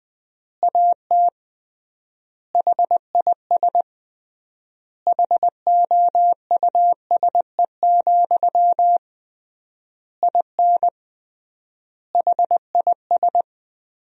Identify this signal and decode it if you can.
Morse code, heard as a keyed tone, about 20 wpm, 700 Hz. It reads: AT HIS HOUSE, IN HIS
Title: 3x Word-Spacing - Sentences from Top 300 Words - 20wpm